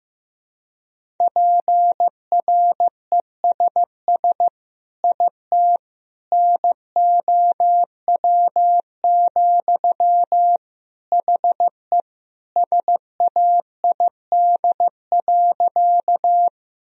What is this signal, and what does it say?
Morse code, heard as a keyed tone, about 16 wpm, 700 Hz. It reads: PRESS IT NOW, HE SAID.
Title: Sentences from Top 1000 Words - Review Part 1 - 15wpm